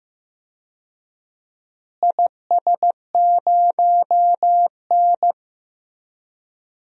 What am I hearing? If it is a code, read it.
Morse code, heard as a keyed tone, about 15 wpm, 700 Hz. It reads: IS0N
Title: DXCC-100 Call Signs - 4 Characters - Repeated 2x - 15wpm